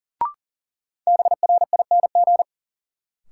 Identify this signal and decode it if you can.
Morse code, heard as a keyed tone, about 40 wpm, 700 Hz. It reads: BRING
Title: Sets of 5 Words Encore - New Word Drill - 40wpm